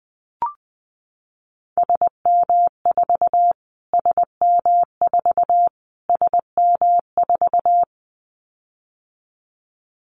Morse code, heard as a keyed tone, about 20 wpm, 700 Hz. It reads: SM4 SM4 SM4
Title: Callsign Prefixes for Top 25 DXCC - Repeated 3x - 20wpm